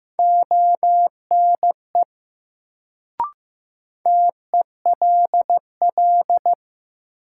Morse code, heard as a keyed tone, about 15 wpm, 700 Hz. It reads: ONE TELL
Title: Sets of 2 Words - New Word Drill - 15wpm